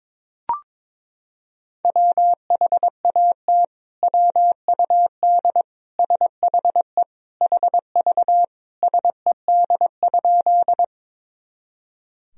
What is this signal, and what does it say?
Morse code, heard as a keyed tone, about 22 wpm, 700 Hz. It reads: WHAT WUD SHE HV SED?
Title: Sentences from Top 100 Words and Abbreviations 22wpm